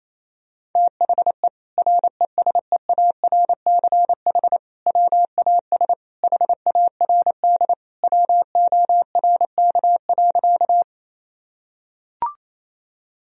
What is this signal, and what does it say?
Morse code, heard as a keyed tone, about 28 wpm, 700 Hz. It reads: THE RESEARCH WAS HARD WORK.
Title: Sentences from Top 1000 Words - Review Part 4 - 28wpm